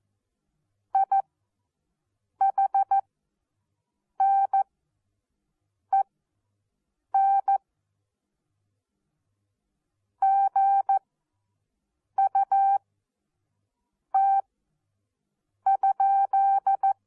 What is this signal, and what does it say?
Morse code, heard as a keyed tone, about 14 wpm, spaced as for 6 wpm, 780 Hz. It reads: IHNEN GUT?
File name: Onlinelehrgang / morsen / DJ4UF Morsen14